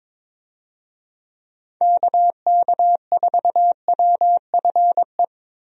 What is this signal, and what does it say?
Morse code, heard as a keyed tone, about 22 wpm, 700 Hz. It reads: KK4WFE
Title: USA Call Signs - 6 Characters - 22wpm